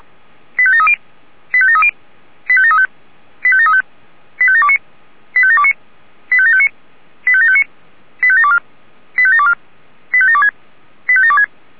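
Five-tone selective call (ZVEI), heard as quick runs of five tones, tones of about 70 ms. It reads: 85730 85730 85735 85735 85729 85729 85759 85759 85724 85724 85726 85726
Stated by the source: ZVEI five-tone selective call received by SDR.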